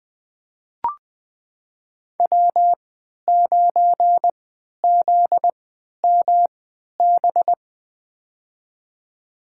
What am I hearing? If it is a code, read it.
Morse code, heard as a keyed tone, about 21 wpm, spaced as for 12 wpm, 700 Hz. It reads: W9ZMB
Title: USA Call Signs - 5 Characters - 3x Character Spacing - 20wpm